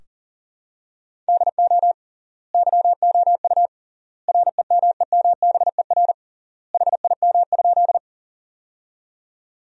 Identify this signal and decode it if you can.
Morse code, heard as a keyed tone, about 40 wpm, 700 Hz. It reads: DO YOU REMEMBER HIM?
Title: 3x Word-Spacing - Sentences from Top 400 Words - 40wpm